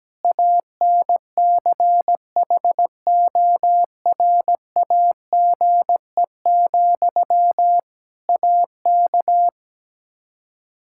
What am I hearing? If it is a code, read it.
Morse code, heard as a keyed tone, about 17 wpm, 700 Hz. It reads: ANCHORAGE, AK
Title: K1USN SST QSO Practice 17wpm